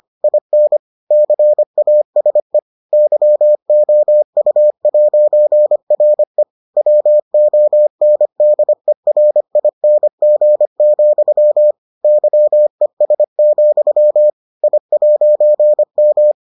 Morse code, heard as a keyed tone, about 25 wpm, 600 Hz. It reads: IN CASE YOU'RE WONDERING, YES, I'M